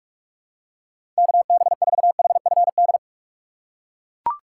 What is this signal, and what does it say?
Morse code, heard as a keyed tone, about 41 wpm, 700 Hz. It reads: KB4HFD
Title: USA Call Signs - 6 Characters - 2x Character Spacing - 45wpm